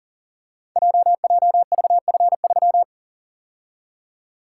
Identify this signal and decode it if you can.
Morse code, heard as a keyed tone, about 42 wpm, 690 Hz. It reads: JJVF3
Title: Code Groups - 5 Letter:Numbers - 40wpm